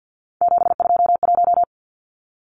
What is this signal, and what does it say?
Morse code, heard as a keyed tone, about 50 wpm, 700 Hz. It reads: 721
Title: Numbers - 3-Digit 50wpm